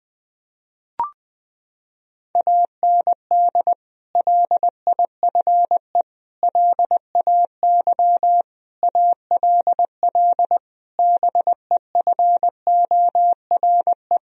Morse code, heard as a keyed tone, about 20 wpm, 700 Hz. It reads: AND LIFE LAY ALL BEFORE
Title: Sentences from Top 1000 Words - Review Part 2 - 20wpm